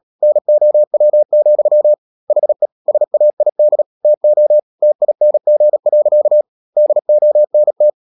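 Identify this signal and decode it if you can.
Morse code, heard as a keyed tone, about 37 wpm, 600 Hz. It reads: NOW, HE SAID TO TING. DONT
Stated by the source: Morse code practice at 35–39 wpm sending English text of Ken Follett's novel Never, from CW med Gnister.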